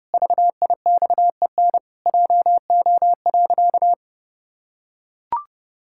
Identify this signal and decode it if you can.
Morse code, heard as a keyed tone, about 30 wpm, 700 Hz. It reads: VIXEN JO.
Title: Pangram Challenge v2 - 30wpm